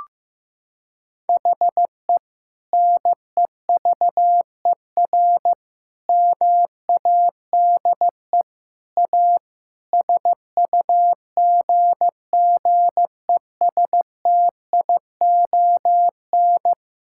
Morse code, heard as a keyed tone, about 16 wpm, 700 Hz. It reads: HE NEVER MADE A SUGGESTION
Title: Sentences from Top 1600 Words 15wpm